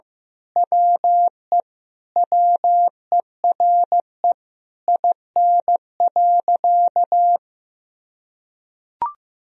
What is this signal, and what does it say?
Morse code, heard as a keyed tone, about 15 wpm, 700 Hz. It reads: WE WERE IN.
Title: Sentences from Top 1000 Words - Review Part 3 - 15wpm